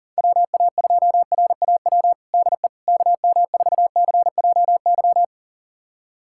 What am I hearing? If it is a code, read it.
Morse code, heard as a keyed tone, about 40 wpm, 700 Hz. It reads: WA2RAW DE KM4CJY
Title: USA Callsign DE Callsign - 6 Characters - 40wpm